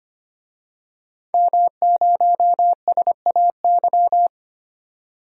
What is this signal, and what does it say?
Morse code, heard as a keyed tone, about 25 wpm, 690 Hz. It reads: M0SAY
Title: DXCC-100 Call Signs - 5 Characters - 25wpm